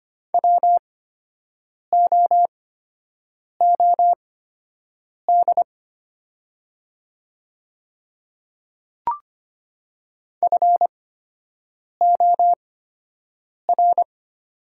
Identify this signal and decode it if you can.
Morse code, heard as a keyed tone, about 25 wpm, spaced as for 7 wpm, 690 Hz. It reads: WOOD FOR
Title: 8x Character Spacing - Top 300 Words - 25wpm